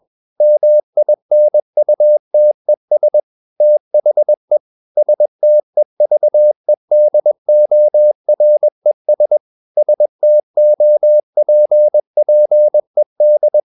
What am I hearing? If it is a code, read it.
Morse code, heard as a keyed tone, about 21 wpm, 600 Hz. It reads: MINUTES THE STEVEDORES STOPPED
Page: CW med Gnister